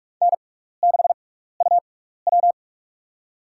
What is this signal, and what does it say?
Morse code, heard as a keyed tone, about 46 wpm, spaced as for 15 wpm, 700 Hz. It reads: N6UW
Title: USA Call Signs - 6x Character Spacing - 45wpm